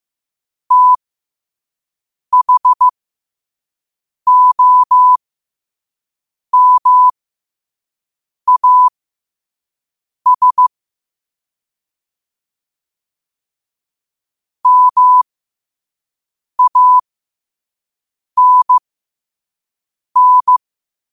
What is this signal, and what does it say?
Morse code, heard as a keyed tone, about 15 wpm, spaced as for 5 wpm, 1000 Hz. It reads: THOMAS MANN
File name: qotdfiles / 5wpm